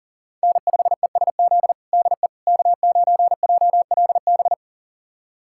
Morse code, heard as a keyed tone, about 40 wpm, 700 Hz. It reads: N5ESZ DE K9JLB
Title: USA Callsign DE Callsign - 5 Characters - 40wpm